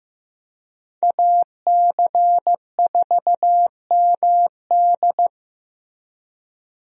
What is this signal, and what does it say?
Morse code, heard as a keyed tone, about 15 wpm, 700 Hz. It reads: AC4MD
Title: USA Call Signs - 5 Characters - Repeated 3x - 15wpm